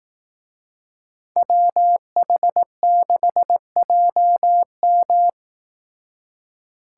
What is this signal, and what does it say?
Morse code, heard as a keyed tone, about 18 wpm, 700 Hz. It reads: WH6JM
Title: Speed Racing - USA Call Signs 15wpm